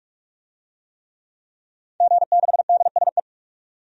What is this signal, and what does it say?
Morse code, heard as a keyed tone, about 45 wpm, 700 Hz. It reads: G6DSE
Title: DXCC-100 Call Signs - 5 Characters - 45wpm